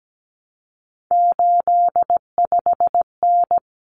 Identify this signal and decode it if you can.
Morse code, heard as a keyed tone, about 17 wpm, 700 Hz. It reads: 85N
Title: Common Cut Numbers - 3-Digit 17wpm